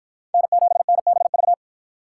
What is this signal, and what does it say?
Morse code, heard as a keyed tone, about 49 wpm, 700 Hz. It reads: N7NBV